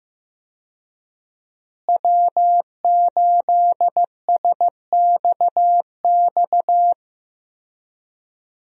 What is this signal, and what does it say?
Morse code, heard as a keyed tone, about 15 wpm, 700 Hz. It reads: W8SXX
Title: USA Call Signs - 5 Characters - Repeated 3x - 15wpm